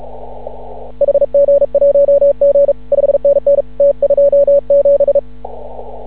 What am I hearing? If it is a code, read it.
Morse code, heard as a keyed tone, about 36 wpm, 560 Hz. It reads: HG1G 5NN T27
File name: CW / ha6oi 2